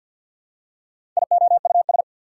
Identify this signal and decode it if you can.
Morse code, heard as a keyed tone, about 53 wpm, 690 Hz. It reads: IOUS
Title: Prefixes and Suffixes 50wpm